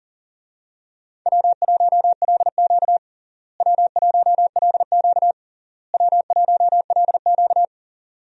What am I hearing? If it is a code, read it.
Morse code, heard as a keyed tone, about 40 wpm, 700 Hz. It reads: W1LQ W1LQ W1LQ
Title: USA Call Signs - 4 Characters - Repeated 3x - 40wpm